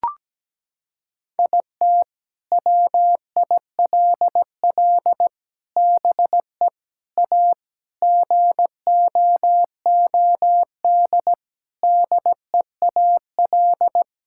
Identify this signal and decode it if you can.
Morse code, heard as a keyed tone, about 17 wpm, 700 Hz. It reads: IT WILL BE A GOOD DEAL
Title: Sentences from Top 1000 Words - Review Part 6 - 17wpm